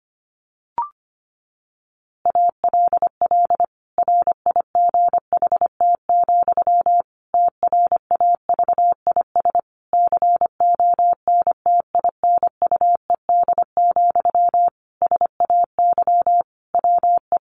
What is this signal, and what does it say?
Morse code, heard as a keyed tone, about 25 wpm, 700 Hz. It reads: ALL RIGHT, TRAVIS CONTINUED, SAY WE
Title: A Sound Of Thunder 25wpm